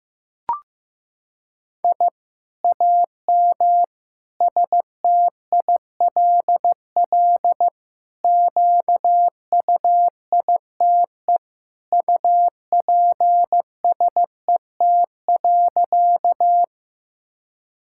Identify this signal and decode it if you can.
Morse code, heard as a keyed tone, about 15 wpm, 700 Hz. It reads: I AM STILL QUITE UPSET.
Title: Sentences from Top 1000 Words - Review Part 5 - 15wpm